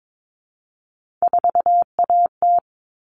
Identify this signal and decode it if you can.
Morse code, heard as a keyed tone, about 22 wpm, 700 Hz. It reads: 4AT
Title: All Cut Numbers - 3-Digit 22wpm